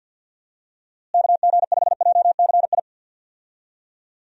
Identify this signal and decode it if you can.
Morse code, heard as a keyed tone, about 50 wpm, 700 Hz. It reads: KG5JCI
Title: USA Call Signs - 6 Characters - Repeated 3x - 50wpm